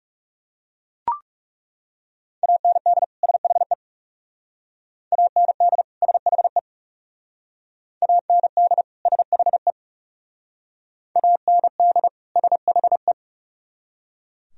Morse code, heard as a keyed tone, about 37 wpm, 700 Hz. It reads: AND SHE AND SHE AND SHE AND SHE
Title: Speed-Racing - Sets of 2 Words 30wpm